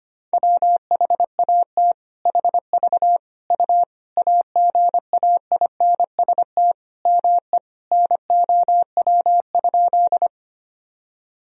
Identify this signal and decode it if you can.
Morse code, heard as a keyed tone, about 25 wpm, 700 Hz. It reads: WHAT HV U AGAINST ME NOW?
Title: Sentences from Top 400 Words and Abbreviations 25wpm